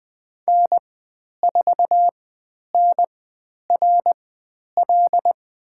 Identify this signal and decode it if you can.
Morse code, heard as a keyed tone, about 20 wpm, spaced as for 10 wpm, 700 Hz. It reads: N4NRL
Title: 095 Lesson 16g - Letter c - Call Signs 20wpm